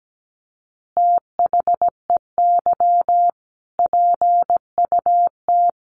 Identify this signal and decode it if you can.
Morse code, heard as a keyed tone, about 17 wpm, 700 Hz. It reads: THEY PUT
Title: Easy Phrases 17wpm